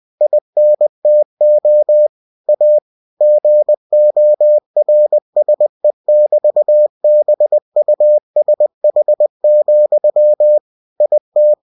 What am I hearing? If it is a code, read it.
Morse code, heard as a keyed tone, about 20 wpm, 600 Hz. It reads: INTO A GORSE<BT>BUSH, IT
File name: cw-books / winnie-the-pooh / winnie-20wpm-page-0004